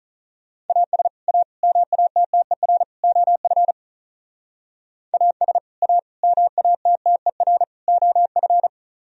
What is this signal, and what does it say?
Morse code, heard as a keyed tone, about 42 wpm, 700 Hz. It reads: AS A MATTER OF AS A MATTER OF
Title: Speed-Racing - Sets of 4 Words 35wpm